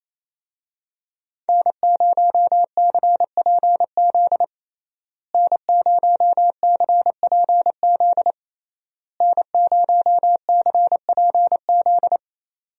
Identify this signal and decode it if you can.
Morse code, heard as a keyed tone, about 28 wpm, 700 Hz. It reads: N0CPZ N0CPZ N0CPZ
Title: USA Call Signs - 5 Characters - Repeated 3x - 28wpm